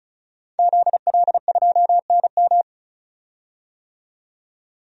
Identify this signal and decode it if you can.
Morse code, heard as a keyed tone, about 36 wpm, 700 Hz. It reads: ZL2NM
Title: DXCC-100 Call Signs - 5 Characters - 35wpm